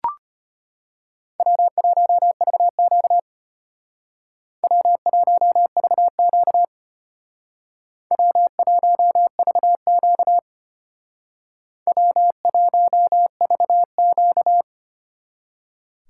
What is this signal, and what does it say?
Morse code, heard as a keyed tone, about 33 wpm, 700 Hz. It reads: W1VQ W1VQ W1VQ W1VQ